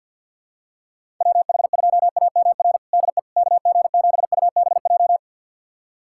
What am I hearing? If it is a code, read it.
Morse code, heard as a keyed tone, about 51 wpm, 700 Hz. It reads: WH2AMR DE KG7UBJ
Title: USA Callsign DE Callsign - 6 Characters - 50wpm